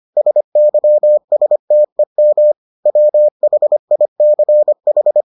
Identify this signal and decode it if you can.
Morse code, heard as a keyed tone, about 25 wpm, 600 Hz. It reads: SYSTEM WHICH